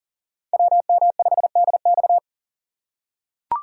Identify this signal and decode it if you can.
Morse code, heard as a keyed tone, about 40 wpm, 700 Hz. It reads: WM5DX
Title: K1USN SST QSO Practice - 40wpm